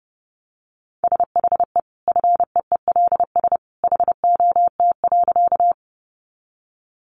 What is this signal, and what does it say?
Morse code, heard as a keyed tone, about 30 wpm, 700 Hz. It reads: SHE FEELS HOT.
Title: Easy Phrases 30wpm